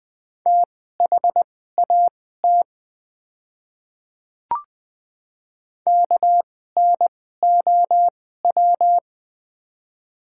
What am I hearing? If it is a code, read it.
Morse code, heard as a keyed tone, about 20 wpm, spaced as for 15 wpm, 700 Hz. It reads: THAT KNOW
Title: 2x Character Spacing - Top 100 Words - 20wpm